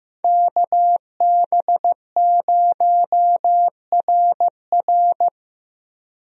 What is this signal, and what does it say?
Morse code, heard as a keyed tone, about 15 wpm, 700 Hz. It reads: KB0RR
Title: USA Call Signs - 5 Characters - 15wpm